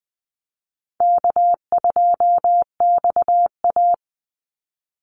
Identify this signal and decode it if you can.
Morse code, heard as a keyed tone, about 20 wpm, 700 Hz. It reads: K2XA
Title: Call Signs - 20wpm